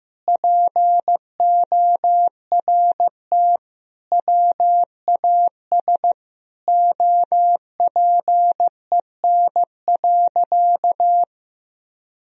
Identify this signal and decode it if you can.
Morse code, heard as a keyed tone, about 16 wpm, 700 Hz. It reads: PORT WAS OPEN.
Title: Sentences from Top 1000 Words - Review Part 1 - 15wpm